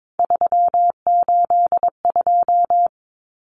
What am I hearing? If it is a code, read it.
Morse code, heard as a keyed tone, about 22 wpm, 700 Hz. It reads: 382
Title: Common Cut Numbers - 3-Digit 22wpm